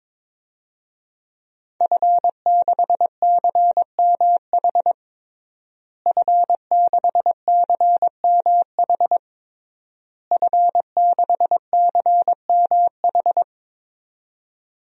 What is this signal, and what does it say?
Morse code, heard as a keyed tone, about 22 wpm, 700 Hz. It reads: F6CMH F6CMH F6CMH
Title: DXCC-100 Call Signs - 5 Characters - Repeated 3x - 22wpm